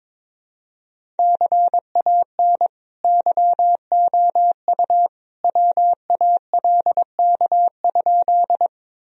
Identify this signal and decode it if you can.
Morse code, heard as a keyed tone, about 22 wpm, 700 Hz. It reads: CAN YOU WALK?
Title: Sentences from Top 1000 Words - Review Part 1 - 22wpm